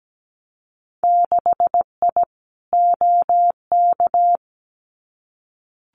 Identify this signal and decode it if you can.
Morse code, heard as a keyed tone, about 17 wpm, 700 Hz. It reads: 6I OK
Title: Winter Field Day Exchanges Encore 17wpm